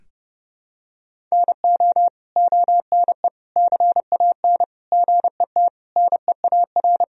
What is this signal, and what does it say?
Morse code, heard as a keyed tone, about 30 wpm, 700 Hz. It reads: NO ONE CAN GET NEAR
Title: Sentences from Top 1000 Words - Review Part 1 - 30wpm